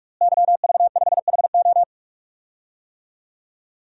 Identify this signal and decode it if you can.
Morse code, heard as a keyed tone, about 42 wpm, 700 Hz. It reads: YV5HO
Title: DXCC-100 Call Signs - 5 Characters - Repeated 2x - 45wpm